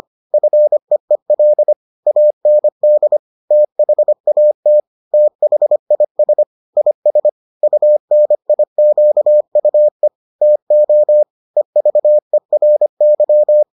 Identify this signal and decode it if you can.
Morse code, heard as a keyed tone, about 25 wpm, 600 Hz. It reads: FEEL AND THAT THIS IS UNIQUE TO EVERY